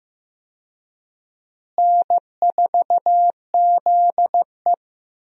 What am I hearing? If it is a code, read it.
Morse code, heard as a keyed tone, about 15 wpm, 700 Hz. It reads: N4ZE
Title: USA Call Signs - 4 Characters - Repeated 3x - 15wpm